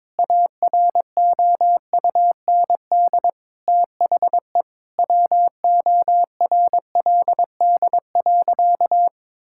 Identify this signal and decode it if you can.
Morse code, heard as a keyed tone, about 22 wpm, 700 Hz. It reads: AROUND THE WORLD.
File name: practice / 22wpm / Sentences with 8-Word Phrases - 22wpm